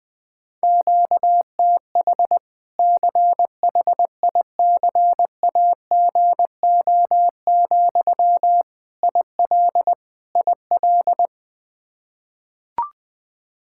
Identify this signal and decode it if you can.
Morse code, heard as a keyed tone, about 20 wpm, 700 Hz. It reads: QTH CHICAGO, IL IL
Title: Practice QSO - Exchange 1 - Beginning - 20wpm